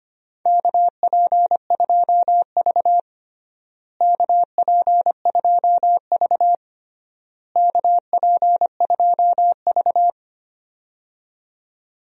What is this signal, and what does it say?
Morse code, heard as a keyed tone, about 25 wpm, 700 Hz. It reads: KP2V KP2V KP2V
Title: DXCC-100 Call Signs - 4 Characters - Repeated 3x - 25wpm